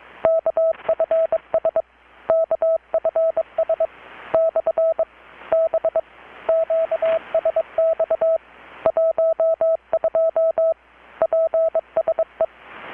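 Morse code, heard as a keyed tone, about 22 wpm, 650 Hz, with radio noise. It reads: KFS KFS / B QSX 12 PSE
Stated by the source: Morse from KPH's Night of Nights XX maritime radiotelegraphy event, received in Chiapas, Mexico about 3,800 km away on a Kenwood R-600 with a grounded 10 m long-wire antenna.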